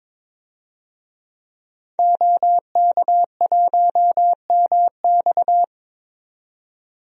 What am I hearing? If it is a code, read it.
Morse code, heard as a keyed tone, about 22 wpm, 700 Hz. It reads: OK1MX
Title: DXCC-100 Call Signs - 5 Characters - 22wpm